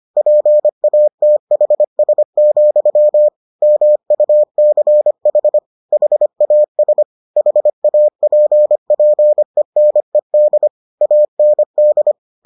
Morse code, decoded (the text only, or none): PATHS, MUCH HAS HAPPENED AND